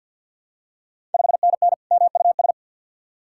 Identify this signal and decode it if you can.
Morse code, heard as a keyed tone, about 53 wpm, 700 Hz. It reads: 5NN MUS
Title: Ontario QSO Party Exchange 50wpm